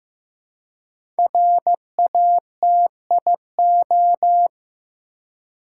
Morse code, heard as a keyed tone, about 15 wpm, 700 Hz. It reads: RATIO
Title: Five Letter Words - 15wpm